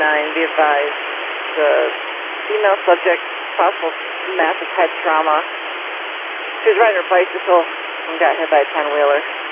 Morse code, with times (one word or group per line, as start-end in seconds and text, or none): none